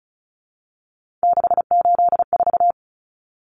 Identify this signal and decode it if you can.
Morse code, heard as a keyed tone, about 34 wpm, 700 Hz. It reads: 684
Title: Numbers - 3-Digit 35wpm